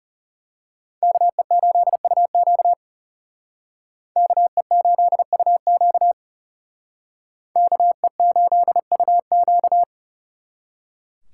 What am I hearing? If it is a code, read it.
Morse code, heard as a keyed tone, about 35 wpm, 700 Hz. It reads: KE8UQ KE8UQ KE8UQ
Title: Speed Racing - USA Call Signs 30wpm